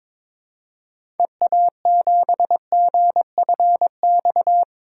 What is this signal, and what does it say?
Morse code, heard as a keyed tone, about 22 wpm, 700 Hz. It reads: EA7GFX
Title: DXCC-100 Call Signs - 6 Characters - 22wpm